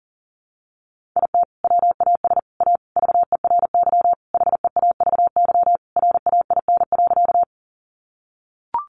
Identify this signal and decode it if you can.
Morse code, heard as a keyed tone, about 40 wpm, 700 Hz. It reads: IT WAS A VERY HEAVY RAIN.
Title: Sentences from Top 1100 Words 40wpm